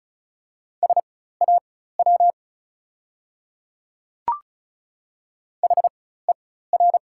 Morse code, heard as a keyed tone, about 34 wpm, spaced as for 16 wpm, 690 Hz. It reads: SAW HER
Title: 4x Character Spacing - Top 500 Words - 35wpm